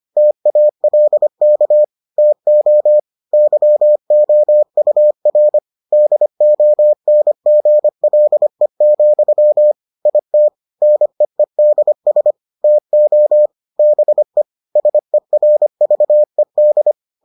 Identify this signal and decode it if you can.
Morse code, heard as a keyed tone, about 25 wpm, 600 Hz. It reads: TALK TO YOUR DONGLE, IT NEEDS TO BE SERVED